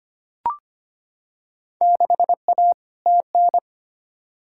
Morse code, heard as a keyed tone, about 25 wpm, 700 Hz. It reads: 6A TN
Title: Field Day Exchanges Encore - 25wpm